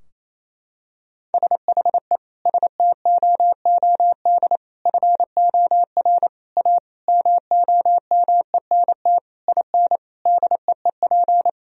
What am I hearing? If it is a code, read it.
Morse code, heard as a keyed tone, about 28 wpm, 700 Hz. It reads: SHE STOOD FOR A MOMENT IN DEEP
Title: Sentences from Top 1000 Words - Review Part 3 - 28wpm